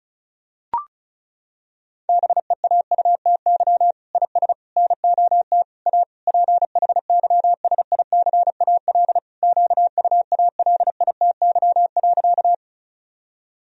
Morse code, heard as a keyed tone, about 36 wpm, 700 Hz. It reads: BEAUTY IS NOT A PHYSICAL QUALITY.
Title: Sentences from Top 1000 Words - Review Part 4 - 35wpm